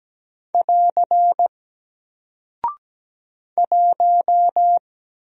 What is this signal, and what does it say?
Morse code, heard as a keyed tone, about 17 wpm, 700 Hz. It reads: <AR> 1
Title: Single Character (Letter:Number:Punctuation:Prosign) Alt 17wpm